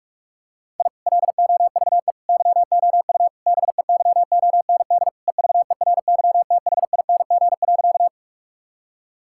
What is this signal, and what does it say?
Morse code, heard as a keyed tone, about 46 wpm, 700 Hz. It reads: I LOVE YOU BEYOND EVERYTHING.
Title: Sentences from Top 1000 Words - Review Part 6 - 45wpm